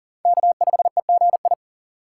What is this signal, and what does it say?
Morse code, heard as a keyed tone, about 40 wpm, 700 Hz. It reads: K5EGI
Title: USA Call Signs - 5 Characters - 6x Character Spacing - 40wpm